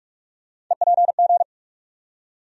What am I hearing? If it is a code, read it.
Morse code, heard as a keyed tone, about 44 wpm, 700 Hz. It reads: EPG
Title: USA Call Sign Postfixes - 3 Letters 45wpm